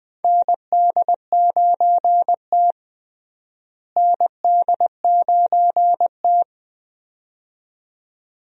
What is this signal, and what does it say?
Morse code, heard as a keyed tone, about 20 wpm, 700 Hz. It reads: ND9T ND9T
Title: USA Call Signs - 4 Characters - Repeated 2x - 20wpm